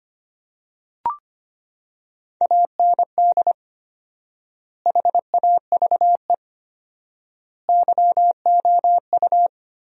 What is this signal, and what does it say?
Morse code, heard as a keyed tone, about 25 wpm, 700 Hz. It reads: AND HAVE YOU
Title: 4x Word-Spacing - Sentences from Top 200 Words - 25wpm